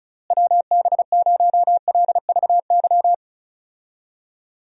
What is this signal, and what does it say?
Morse code, heard as a keyed tone, about 36 wpm, 700 Hz. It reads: WB0LVY
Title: USA Call Signs - 6 Characters - Repeated 2x - 35wpm